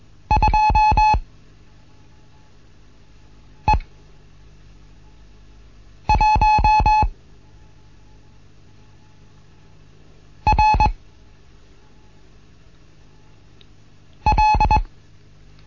Morse code, heard as a keyed tone, about 21 wpm, spaced as for 3 wpm, 860 Hz. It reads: 2E1 R L
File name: Morsen / Kurs A 15